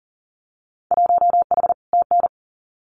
Morse code, heard as a keyed tone, about 38 wpm, 700 Hz. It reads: 1H TN
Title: Winter Field Day Exchanges 40wpm